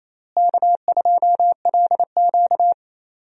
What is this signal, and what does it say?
Morse code, heard as a keyed tone, about 28 wpm, 700 Hz. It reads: K2LQ
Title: USA Call Signs - 4 Characters - Repeated 3x - 28wpm